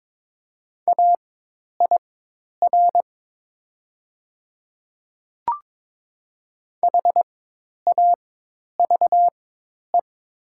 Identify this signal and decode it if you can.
Morse code, heard as a keyed tone, about 22 wpm, spaced as for 10 wpm, 700 Hz. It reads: AIR HAVE
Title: 4x Character Spacing - Top 300 Words - 22wpm